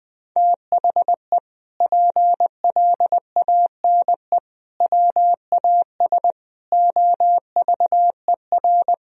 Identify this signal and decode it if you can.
Morse code, heard as a keyed tone, about 20 wpm, 700 Hz. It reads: THE PLANE WAS OVER
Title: Sentences from Top 1000 Words - Review Part 3 - 20wpm